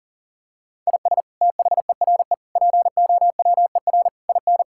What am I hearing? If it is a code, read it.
Morse code, heard as a keyed tone, about 40 wpm, 700 Hz. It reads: IS THERE POWER IN
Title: Sentences from Top 1000 Words - Review Part 6 - 40wpm